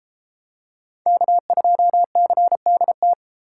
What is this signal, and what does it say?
Morse code, heard as a keyed tone, about 33 wpm, 700 Hz. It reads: K2CDT